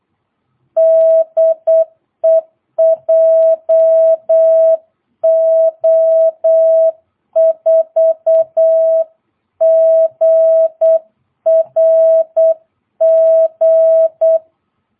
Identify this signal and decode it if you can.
Morse code, heard as a keyed tone, about 8 wpm, 660 Hz. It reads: DE JO4GRG
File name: jh8cbh / 2022cw